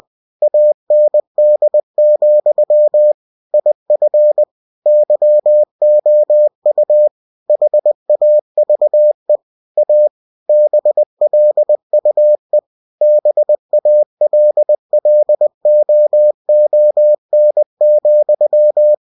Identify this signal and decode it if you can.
Morse code, heard as a keyed tone, about 20 wpm, 600 Hz. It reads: AND, IF YOU HAVE A BLUE BALLOON,
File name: cw-books / winnie-the-pooh / winnie-20wpm-page-0005